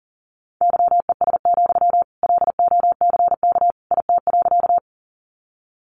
Morse code, heard as a keyed tone, about 37 wpm, 700 Hz. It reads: YES, LOCK IT.
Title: Sentences from Top 1300 Words 40wpm